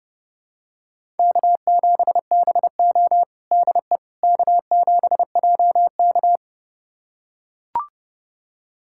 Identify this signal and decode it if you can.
Morse code, heard as a keyed tone, about 30 wpm, 700 Hz. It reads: K7BO DE K7JK
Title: USA Callsign DE Callsign - 4 Characters - 30wpm